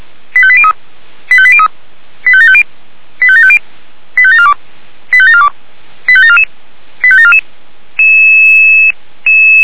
Five-tone selective call (ZVEI), heard as quick runs of five tones, tones of about 70 ms. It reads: 85803 85803 85750 85750 85742 85742 85744 85744
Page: /464/SDR/ZVEI/